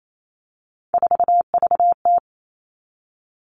Morse code, heard as a keyed tone, about 28 wpm, 700 Hz. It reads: 4VT
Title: All Cut Numbers - 3-Digit 28wpm